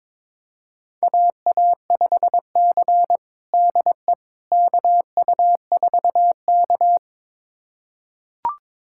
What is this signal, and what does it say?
Morse code, heard as a keyed tone, about 21 wpm, 700 Hz. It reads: AA5C DE KU4K
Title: USA Callsign DE Callsign - 4 Characters - 22wpm